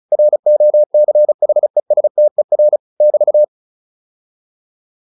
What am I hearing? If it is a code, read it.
Morse code, heard as a keyed tone, about 34 wpm, 600 Hz. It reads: ROCHESTER <BT>
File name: QSO 402 35